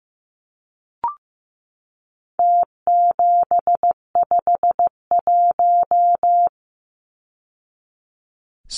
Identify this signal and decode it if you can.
Morse code, heard as a keyed tone, about 15 wpm, 700 Hz. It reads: T751
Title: Numbers - Contest Style - 4-Digit 15wpm